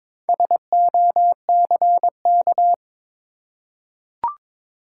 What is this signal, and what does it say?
Morse code, heard as a keyed tone, about 22 wpm, 700 Hz. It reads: SOCK
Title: Four Letter Words - 22wpm